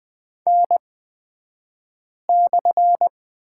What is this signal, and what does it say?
Morse code, heard as a keyed tone, about 20 wpm, spaced as for 5 wpm, 700 Hz. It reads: N/
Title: 228 Lesson 37d - Letter x - Two Characters 20wpm